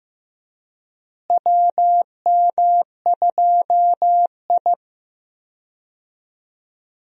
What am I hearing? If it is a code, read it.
Morse code, heard as a keyed tone, about 15 wpm, 700 Hz. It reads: WM2I
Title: USA Call Signs - 4 Characters - Repeated 2x - 15wpm